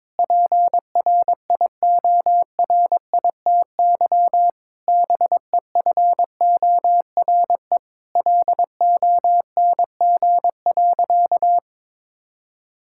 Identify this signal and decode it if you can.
Morse code, heard as a keyed tone, about 22 wpm, 700 Hz. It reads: PRIORITY BEFORE LONG.
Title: Sentences from Top 1600 Words 22wpm